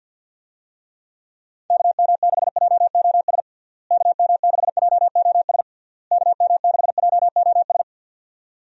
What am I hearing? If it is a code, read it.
Morse code, heard as a keyed tone, about 47 wpm, 700 Hz. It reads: KM6JOS KM6JOS KM6JOS
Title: USA Call Signs - 6 Characters - Repeated 3x - 50wpm